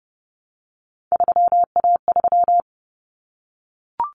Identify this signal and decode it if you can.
Morse code, heard as a keyed tone, about 30 wpm, 700 Hz. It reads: 3A3
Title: Common Cut Numbers - 3-Digit 30wpm